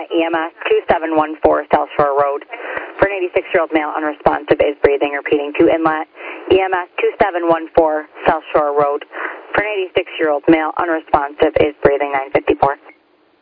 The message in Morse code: none

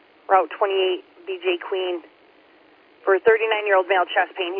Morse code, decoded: none

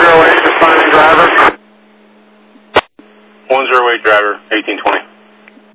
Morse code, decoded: none

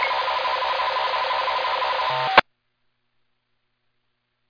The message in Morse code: none

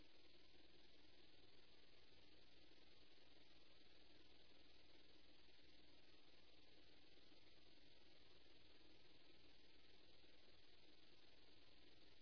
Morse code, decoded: none

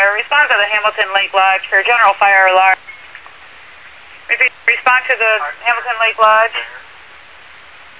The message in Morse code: none

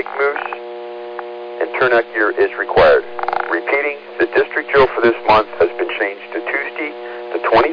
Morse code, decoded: none

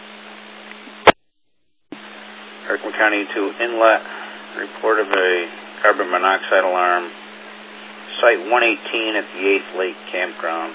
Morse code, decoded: none

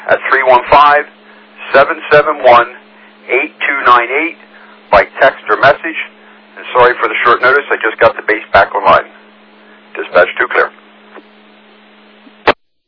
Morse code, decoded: none

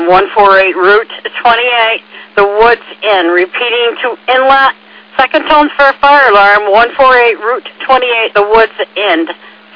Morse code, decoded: none

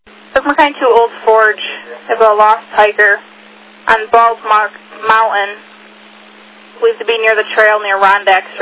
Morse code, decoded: none

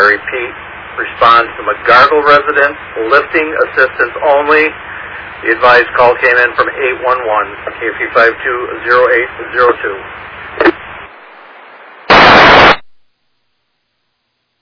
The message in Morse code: none

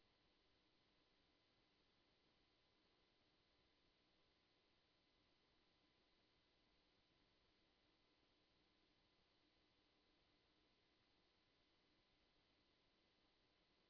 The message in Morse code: none

